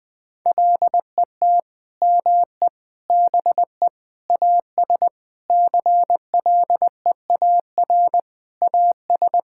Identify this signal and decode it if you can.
Morse code, heard as a keyed tone, about 20 wpm, 700 Hz. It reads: LET ME BE AS CLEAR AS
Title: Sentences with 9-Word Phrases - 20wpm